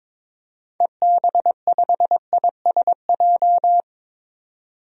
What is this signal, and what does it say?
Morse code, heard as a keyed tone, about 22 wpm, 700 Hz. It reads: EB5ISJ
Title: DXCC-100 Call Signs - 6 Characters - 22wpm